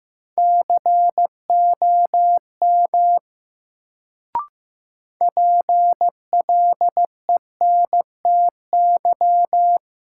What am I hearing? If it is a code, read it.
Morse code, heard as a keyed tone, about 15 wpm, 700 Hz. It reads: COM PLENTY
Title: Sentences from Top 1300 Words - New Word Drill - 15wpm